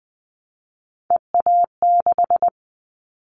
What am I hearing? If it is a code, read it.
Morse code, heard as a keyed tone, about 20 wpm, 700 Hz. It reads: EA6
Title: Callsign Prefixes for Top 50 DXCC - 20wpm